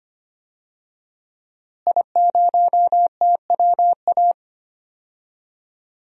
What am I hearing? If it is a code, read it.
Morse code, heard as a keyed tone, about 25 wpm, 700 Hz. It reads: I0TWA
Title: DXCC-100 Call Signs - 5 Characters - Repeated 3x - 25wpm